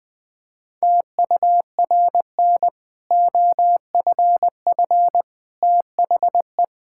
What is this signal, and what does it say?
Morse code, heard as a keyed tone, about 20 wpm, 700 Hz. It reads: TURN OFF THE
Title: Sentences from Top 1000 Words - Review Part 6 - 20wpm